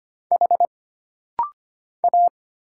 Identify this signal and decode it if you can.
Morse code, heard as a keyed tone, about 25 wpm, 700 Hz. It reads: H A
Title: Single Letters - Rapid-Fire 25wpm